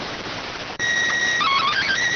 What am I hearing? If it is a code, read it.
SSTV (slow-scan television), PD 290.